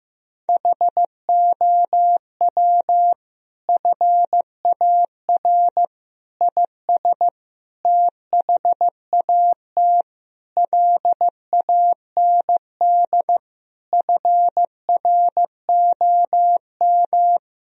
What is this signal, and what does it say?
Morse code, heard as a keyed tone, about 15 wpm, 700 Hz. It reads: HOW FAR IS THAT LAND FROM
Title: Sentences from Top 1000 Words - Review Part 1 - 15wpm